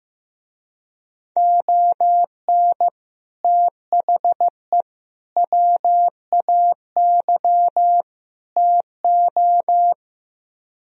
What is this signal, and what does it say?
Morse code, heard as a keyed tone, about 15 wpm, 700 Hz. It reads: ON THE WAY TO
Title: Speed-Racing - Sets of 4 Words 15wpm